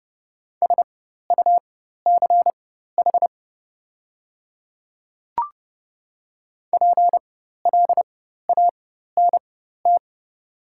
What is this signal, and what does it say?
Morse code, heard as a keyed tone, about 30 wpm, spaced as for 14 wpm, 690 Hz. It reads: SUCH PLANT
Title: 4x Character Spacing - Top 500 Words - 30wpm